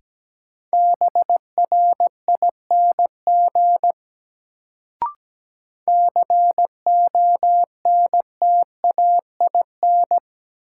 Morse code, heard as a keyed tone, about 17 wpm, 700 Hz. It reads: BRING CONTAIN
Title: Sentences from Top 500 Words - New Word Drill - 17wpm